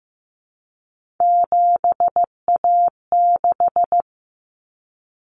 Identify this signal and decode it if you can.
Morse code, heard as a keyed tone, about 15 wpm, 700 Hz. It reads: 7A6
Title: Common Cut Numbers - 3-Digit 15wpm